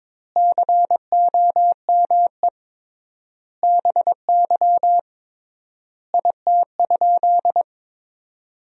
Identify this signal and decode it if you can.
Morse code, heard as a keyed tone, about 22 wpm, 700 Hz. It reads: COME BY IT?
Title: 3x Word-Spacing - Sentences from Top 100 Words - 22wpm